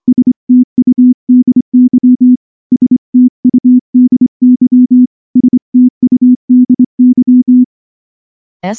Morse code, heard as a keyed tone, about 23 wpm, 250 Hz. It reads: STUDY STUDY STUDY
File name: morse / morse-training-25WPM